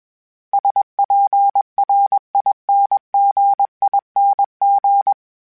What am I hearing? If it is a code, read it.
Morse code, heard as a keyed tone, about 21 wpm, 800 Hz. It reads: SPRINGING
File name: Special INGs / Special INGs-mp3s / Sngl Ing @20